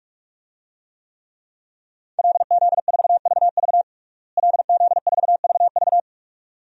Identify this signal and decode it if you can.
Morse code, heard as a keyed tone, about 42 wpm, 700 Hz. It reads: LZ4VV LZ4VV
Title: DXCC-100 Call Signs - 5 Characters - Repeated 2x - 45wpm